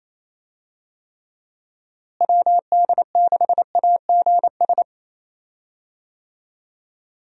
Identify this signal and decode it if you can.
Morse code, heard as a keyed tone, about 28 wpm, 700 Hz. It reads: WD6AGS
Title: USA Call Signs - 6 Characters - Repeated 3x - 28wpm